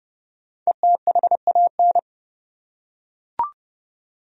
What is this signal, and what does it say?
Morse code, heard as a keyed tone, about 30 wpm, 700 Hz. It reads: ETHAN
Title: USA Common Names - 30wpm